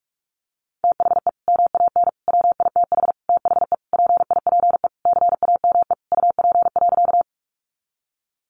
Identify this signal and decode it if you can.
Morse code, heard as a keyed tone, about 44 wpm, 700 Hz. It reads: THE MAN WITH THE PIPE CAME UP.
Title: Sentences from Top 1500 Words 45wpm